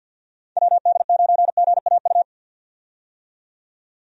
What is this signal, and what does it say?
Morse code, heard as a keyed tone, about 50 wpm, 700 Hz. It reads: WD9GAU
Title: USA Call Signs - 6 Characters - 50wpm